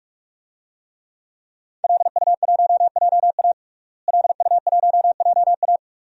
Morse code, heard as a keyed tone, about 46 wpm, 700 Hz. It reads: LU1JA LU1JA
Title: DXCC-100 Call Signs - 5 Characters - Repeated 3x - 45wpm